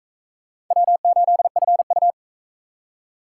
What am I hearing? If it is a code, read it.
Morse code, heard as a keyed tone, about 44 wpm, 700 Hz. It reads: W8FU